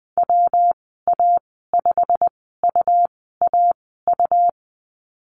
Morse code, heard as a keyed tone, about 20 wpm, spaced as for 14 wpm, 700 Hz. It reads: WA5UAU
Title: 161 Lesson 26g - Letter g - Call Signs 20wpm